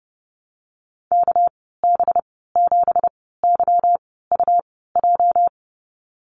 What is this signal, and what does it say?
Morse code, heard as a keyed tone, about 31 wpm, spaced as for 17 wpm, 700 Hz. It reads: KB7YUJ
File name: Lesson 37g - Letter x - Call Signs 30wpm